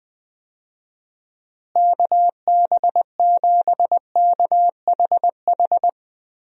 Morse code, heard as a keyed tone, about 20 wpm, 700 Hz. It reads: KB7KHH
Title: USA Call Signs - 6 Characters - Repeated 2x - 20wpm